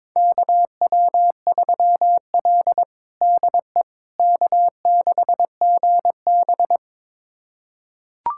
Morse code, heard as a keyed tone, about 21 wpm, 700 Hz. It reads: KW3L DE K6GB
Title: USA Callsign DE Callsign - 4 Characters - 22wpm